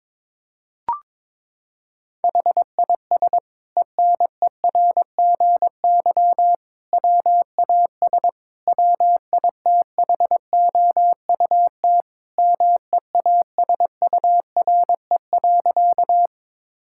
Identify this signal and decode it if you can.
Morse code, heard as a keyed tone, about 22 wpm, 700 Hz. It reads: HIS ENERGY WAS WITHOUT MEASURE.
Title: Sentences from Top 1000 Words - Review Part 6 - 22wpm